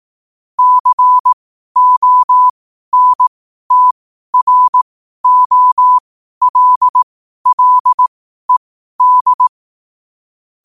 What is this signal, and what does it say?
Morse code, heard as a keyed tone, about 18 wpm, spaced as for 12 wpm, 1000 Hz. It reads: CONTROLLED
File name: qotdfiles / 12wpm